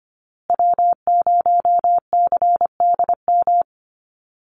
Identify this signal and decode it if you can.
Morse code, heard as a keyed tone, about 25 wpm, 700 Hz. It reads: W0CDM
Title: USA Call Signs Repeated 3x 25wpm